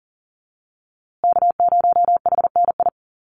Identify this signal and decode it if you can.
Morse code, heard as a keyed tone, about 40 wpm, 700 Hz. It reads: K0HNI
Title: USA Call Signs Repeated 3x 40wpm